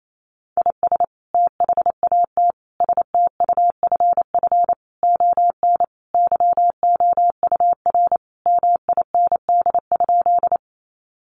Morse code, heard as a keyed tone, about 28 wpm, 700 Hz. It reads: IS THAT STUFF ON YOUR MIND?
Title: Sentences from Top 1400 Words 28wpm